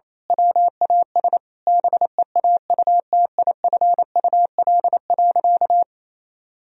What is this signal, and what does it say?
Morse code, heard as a keyed tone, about 27 wpm, 700 Hz. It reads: WAS BEAUTIFUL.
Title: Sentences from Top 1000 Words - Review Part 1 - 28wpm